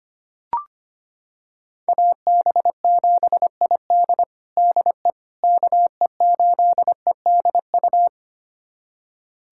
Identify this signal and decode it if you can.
Morse code, heard as a keyed tone, about 25 wpm, 700 Hz. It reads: AB7ID DE KE8EDU
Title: USA Callsign DE Callsign 25wpm